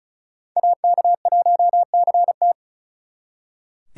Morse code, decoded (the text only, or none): AK1CT